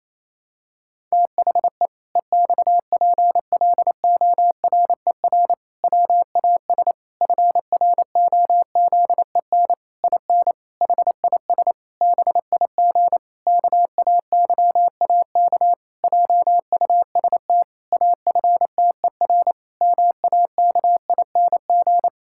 Morse code, heard as a keyed tone, about 28 wpm, 700 Hz. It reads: THE EXPLORER WAS FROZEN IN HIS BIG KAYAK JUST AFTER MAKING
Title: Pangram Challenge v2 - 28wpm